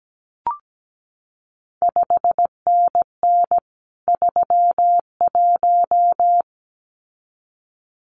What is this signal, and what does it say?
Morse code, heard as a keyed tone, about 17 wpm, 700 Hz. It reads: 5NN 31
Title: CQ WW DX Contest Exchange - 17wpm